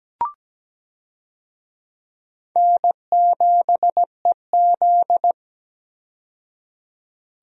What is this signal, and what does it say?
Morse code, heard as a keyed tone, about 17 wpm, 700 Hz. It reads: N7EZ
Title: USA Call Signs - 4 Characters - 17wpm